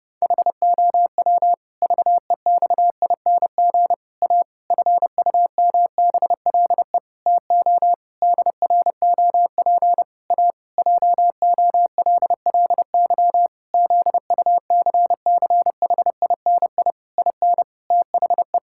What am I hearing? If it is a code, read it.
Morse code, heard as a keyed tone, about 32 wpm, 700 Hz. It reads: HOW VEXING A FUMBLE TO DROP A JOLLY ZUCCHINI IN THE
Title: Pangram Challenge v2 - 30wpm